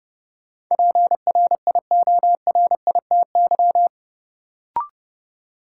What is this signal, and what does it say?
Morse code, heard as a keyed tone, about 30 wpm, 700 Hz. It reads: PRIORITY
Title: Sentences from Top 1600 Words - New Word Drill - 30wpm